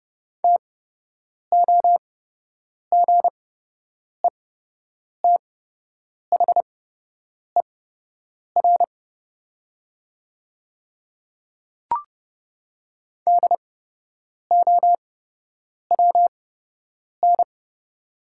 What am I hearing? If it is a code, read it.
Morse code, heard as a keyed tone, about 29 wpm, spaced as for 8 wpm, 700 Hz. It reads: TOGETHER DOWN
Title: 8x Character Spacing - Top 500 Words - 30wpm